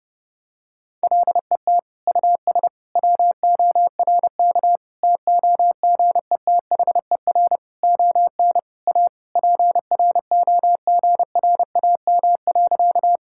Morse code, heard as a keyed tone, about 30 wpm, 700 Hz. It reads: LET US WORK TOGETHER ON A PROGRAM.
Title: Sentences from Top 1000 Words - Review Part 5 - 30wpm